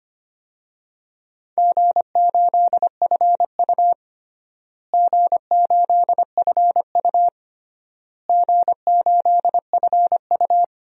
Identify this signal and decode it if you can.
Morse code, heard as a keyed tone, about 25 wpm, 700 Hz. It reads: G8FU G8FU G8FU
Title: DXCC-100 Call Signs - 4 Characters - Repeated 3x - 25wpm